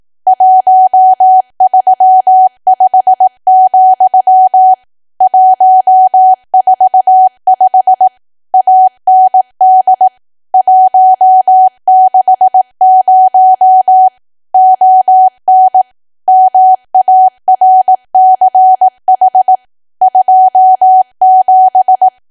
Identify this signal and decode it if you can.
Morse code, heard as a keyed tone, about 18 wpm, 750 Hz. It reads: 135, 145 AND 160 ON MARCH 27
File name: cwa / W1AW-Level2 / 18WPM